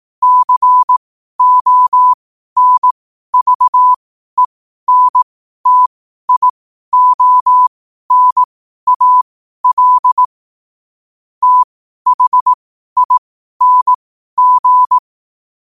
Morse code, heard as a keyed tone, about 18 wpm, spaced as for 12 wpm, 1000 Hz. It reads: CONVENTIONAL THING
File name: qotdfiles / 12wpm